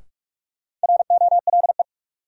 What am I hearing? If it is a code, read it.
Morse code, heard as a keyed tone, about 41 wpm, 700 Hz. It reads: ROLE
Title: Sets of 3 Words - New Word Drill - 45wpm